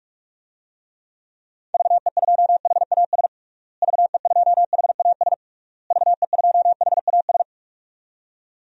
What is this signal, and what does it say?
Morse code, heard as a keyed tone, about 41 wpm, 700 Hz. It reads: VE2HAS VE2HAS VE2HAS
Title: DXCC-100 Call Signs - 6 Characters - Repeated 3x - 45wpm